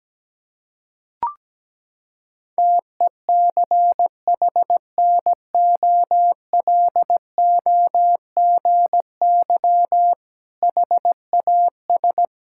Morse code, heard as a keyed tone, about 17 wpm, 700 Hz. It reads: TECHNOLOGY HAS
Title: Sentences from Top 1000 Words - Review Part 6 - 17wpm